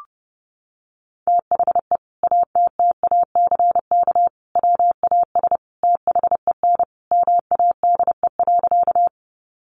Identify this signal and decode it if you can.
Morse code, heard as a keyed tone, about 30 wpm, 700 Hz. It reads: THE ATTACK WAS THEN MADE.
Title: Sentences from Top 1500 Words 30wpm